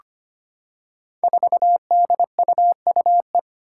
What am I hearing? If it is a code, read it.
Morse code, heard as a keyed tone, about 25 wpm, 700 Hz. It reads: 4DUUE
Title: Five Challenging Letter-Number Combinations 25wpm